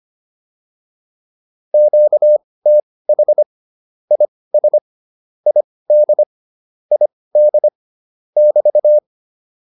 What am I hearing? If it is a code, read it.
Morse code, heard as a keyed tone, about 25 wpm, spaced as for 18 wpm, 600 Hz. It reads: QTH IS ID ID <BT>
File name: QSO 105 18